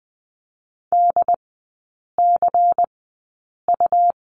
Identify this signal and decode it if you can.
Morse code, heard as a keyed tone, about 19 wpm, spaced as for 8 wpm, 700 Hz. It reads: DCU
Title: 121 Lesson 20e - Number 3 - Three Characters 20wpm